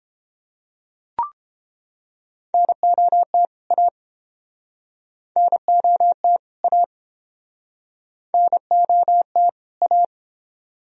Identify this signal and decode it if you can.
Morse code, heard as a keyed tone, about 31 wpm, 700 Hz. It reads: NOT A NOT A NOT A